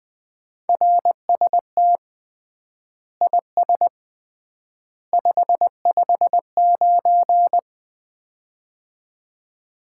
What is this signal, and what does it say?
Morse code, heard as a keyed tone, about 20 wpm, 700 Hz. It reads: RST IS 559
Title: Common QSO Phrases - 3x Word-Spacing - 20wpm